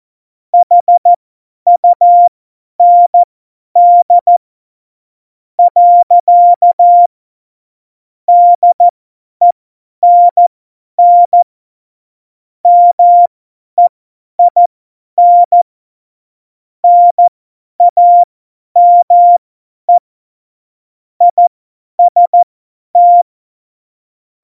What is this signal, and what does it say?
Morse code, heard as a keyed tone, about 14 wpm, spaced as for 10 wpm, 700 Hz. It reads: HUND . DENN MEIN NAME IST
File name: Morsekurs / Witze / Witz 12. Juni 2018, 50 F70